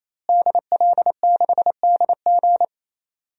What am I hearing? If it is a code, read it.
Morse code, heard as a keyed tone, about 28 wpm, 700 Hz. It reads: DL6DG
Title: DXCC-100 Call Signs - 5 Characters - Repeated 3x - 28wpm